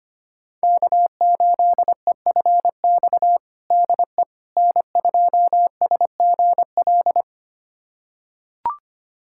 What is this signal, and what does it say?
Morse code, heard as a keyed tone, about 25 wpm, 700 Hz. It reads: K8EFX DE N2SGL
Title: USA Callsign DE Callsign - 5 Characters - 25wpm